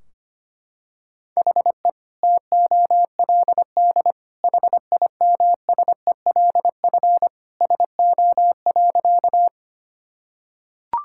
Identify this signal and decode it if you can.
Morse code, heard as a keyed tone, about 25 wpm, 700 Hz. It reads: HE TOLD HIMSELF SO.
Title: Sentences from Top 1600 Words 25wpm